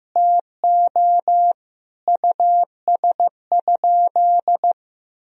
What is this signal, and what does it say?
Morse code, heard as a keyed tone, about 15 wpm, 700 Hz. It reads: TO US?
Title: Sentences from Top 1000 Words - Review Part 3 - 15wpm